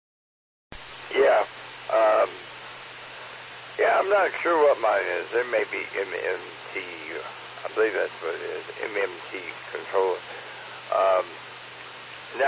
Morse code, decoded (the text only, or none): none